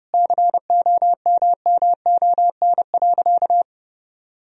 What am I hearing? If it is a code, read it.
Morse code, heard as a keyed tone, about 30 wpm, 700 Hz. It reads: COMMON.
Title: Sentences from Top 1000 Words - Review Part 3 - 30wpm